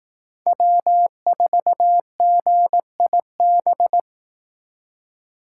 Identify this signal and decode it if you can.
Morse code, heard as a keyed tone, about 18 wpm, 700 Hz. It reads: W4GIB
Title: Speed Racing - USA 1x3 Call Signs 15wpm